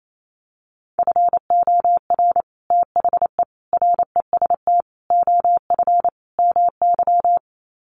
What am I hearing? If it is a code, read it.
Morse code, heard as a keyed tone, about 28 wpm, 700 Hz. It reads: FOR THE REST OF MY
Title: Sets of 5 Words 28wpm